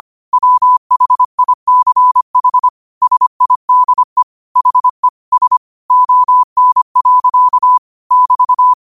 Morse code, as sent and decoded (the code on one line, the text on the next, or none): .-- .... .. -.-. .... / ... .. -.. / .... / ... / --- -. .-.-.- / -...-
WHICH SIDE HE S ON. <BT>